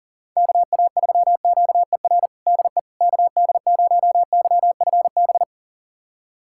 Morse code, heard as a keyed tone, about 37 wpm, 700 Hz. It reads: KA3QER DE KD0YFB
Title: USA Callsign DE Callsign - 6 Characters - 40wpm